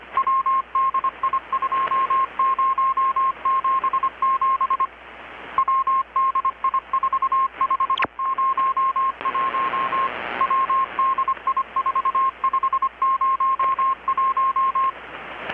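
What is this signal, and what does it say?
Morse code, heard as a keyed tone, about 25 wpm, 1070 Hz. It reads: WDI2077 WDI4501 WDI4501
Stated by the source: Morse from KPH's Night of Nights XX maritime radiotelegraphy event, received in Chiapas, Mexico about 3,800 km away on a Kenwood R-600 with a grounded 10 m long-wire antenna.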